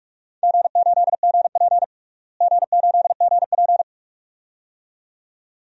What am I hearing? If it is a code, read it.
Morse code, heard as a keyed tone, about 46 wpm, 700 Hz. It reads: G8GP G8GP
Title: DXCC-100 Call Signs - 4 Characters - Repeated 2x - 45wpm